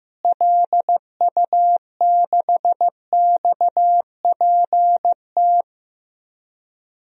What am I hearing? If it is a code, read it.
Morse code, heard as a keyed tone, about 15 wpm, 700 Hz. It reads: LU6XPT
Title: DXCC-100 Call Signs - 6 Characters - Repeated 2x - 15wpm